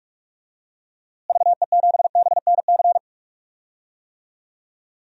Morse code, heard as a keyed tone, about 47 wpm, 700 Hz. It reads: VE7BNC
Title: DXCC-100 Call Signs - 6 Characters - Repeated 2x - 45wpm